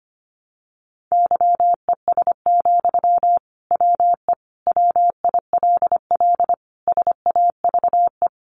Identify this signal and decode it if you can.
Morse code, heard as a keyed tone, about 25 wpm, 700 Hz. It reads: YES, WE WILL SAVE
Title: Sentences from Top 1100 Words 25wpm